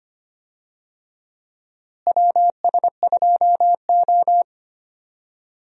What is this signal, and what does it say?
Morse code, heard as a keyed tone, about 25 wpm, 700 Hz. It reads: WS2O
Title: USA Call Signs - 4 Characters - Repeated 3x - 25wpm